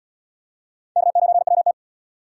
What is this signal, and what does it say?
Morse code, heard as a keyed tone, about 74 wpm, 700 Hz. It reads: K9WT